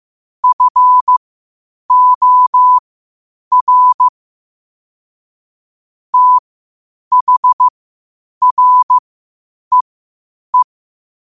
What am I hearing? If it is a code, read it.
Morse code, heard as a keyed tone, about 15 wpm, spaced as for 8 wpm, 1000 Hz. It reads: FOR THREE
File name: qotdfiles / 8wpm